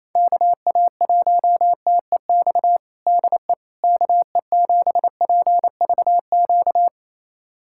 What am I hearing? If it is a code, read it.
Morse code, heard as a keyed tone, about 28 wpm, 700 Hz. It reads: KA1TEX DE KE7PVQ
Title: USA Callsign DE Callsign 28wpm